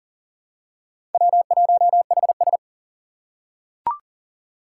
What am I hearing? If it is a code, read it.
Morse code, heard as a keyed tone, about 40 wpm, 700 Hz. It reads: W1HS
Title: USA Call Signs - 4x Character Spacing - 40wpm